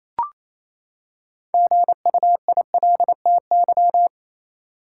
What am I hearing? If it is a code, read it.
Morse code, heard as a keyed tone, about 28 wpm, 700 Hz. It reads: GUILTY
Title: Six Letter Words - 28wpm